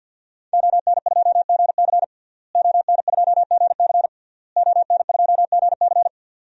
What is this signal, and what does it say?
Morse code, heard as a keyed tone, about 47 wpm, 700 Hz. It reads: ON2GC ON2GC ON2GC
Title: DXCC-100 Call Signs - 5 Characters - Repeated 3x - 50wpm